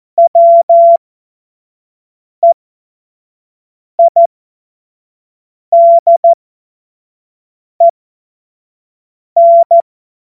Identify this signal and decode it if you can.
Morse code, heard as a keyed tone, about 14 wpm, spaced as for 5 wpm, 680 Hz. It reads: WEIDEN
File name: Witz-23-September-2025-25-F70